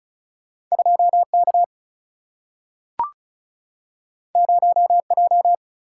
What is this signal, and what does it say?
Morse code, heard as a keyed tone, about 36 wpm, 700 Hz. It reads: 2 K 0 J
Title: Single-Letter Single-Number 35wpm